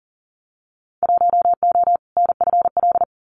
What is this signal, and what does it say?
Morse code, heard as a keyed tone, about 40 wpm, 700 Hz. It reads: 1O NFL
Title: Winter Field Day Exchanges 40wpm